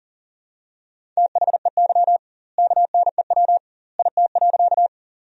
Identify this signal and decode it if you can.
Morse code, heard as a keyed tone, about 40 wpm, 700 Hz. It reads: THEY KNEW IT.
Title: 2x Word-Spacing - Sentences from Top 400 Words - 40wpm